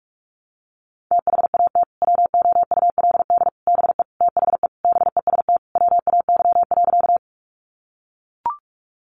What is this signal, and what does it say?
Morse code, heard as a keyed tone, about 44 wpm, 700 Hz. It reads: THAT WOULD BE THE BEST WAY.
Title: Sentences from Top 400 Words 45wpm